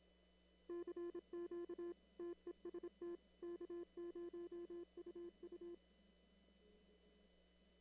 Morse code, heard as a keyed tone, about 26 wpm, 340 Hz. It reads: CQ TEST K0UU
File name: CW / kt5j wpx cw-op-k5pi